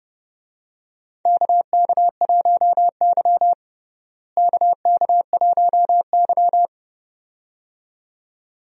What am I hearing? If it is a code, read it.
Morse code, heard as a keyed tone, about 30 wpm, 700 Hz. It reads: KK1Y KK1Y
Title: USA Call Signs - 4 Characters - Repeated 2x - 30wpm